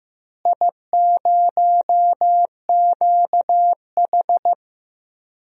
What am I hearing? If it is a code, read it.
Morse code, heard as a keyed tone, about 15 wpm, 700 Hz. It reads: I0QH
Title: DXCC-100 Call Signs - 4 Characters - Repeated 3x - 15wpm